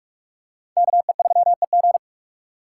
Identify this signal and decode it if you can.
Morse code, heard as a keyed tone, about 43 wpm, 700 Hz. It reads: KE3EG